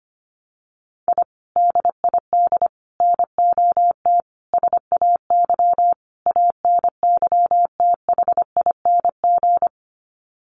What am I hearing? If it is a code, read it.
Morse code, heard as a keyed tone, about 25 wpm, 700 Hz. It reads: I DID NOT SAY ANYTHING
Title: Sets of 5 Words Encore2 25wpm